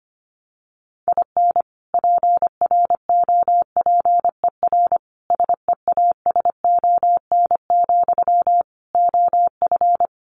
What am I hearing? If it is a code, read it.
Morse code, heard as a keyed tone, about 25 wpm, 700 Hz. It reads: IN PROPER SEASON, OF
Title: Sentences from Top 1400 Words 25wpm